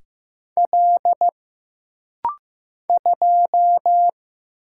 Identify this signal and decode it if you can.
Morse code, heard as a keyed tone, about 15 wpm, 700 Hz. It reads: L 2
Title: Single Letter-Number Alt - Rapid-Fire 15wpm